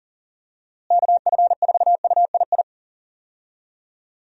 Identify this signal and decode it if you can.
Morse code, heard as a keyed tone, about 39 wpm, 700 Hz. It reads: KF4UII
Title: USA Call Signs - 6 Characters - Repeated 2x - 40wpm